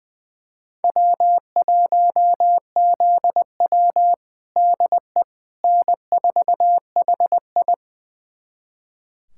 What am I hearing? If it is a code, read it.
Morse code, heard as a keyed tone, about 20 wpm, 700 Hz. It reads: W1ZW DE N4HI
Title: USA Callsign DE Callsign - 4 Characters - 20wpm